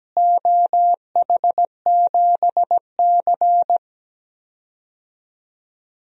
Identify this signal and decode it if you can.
Morse code, heard as a keyed tone, about 17 wpm, 700 Hz. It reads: OH7C
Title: DXCC-100 Call Signs - 4 Characters - Repeated 3x - 17wpm